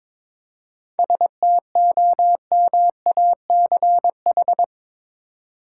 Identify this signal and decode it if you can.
Morse code, heard as a keyed tone, about 22 wpm, 700 Hz. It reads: STOMACH
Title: Seven Letter Words - 22wpm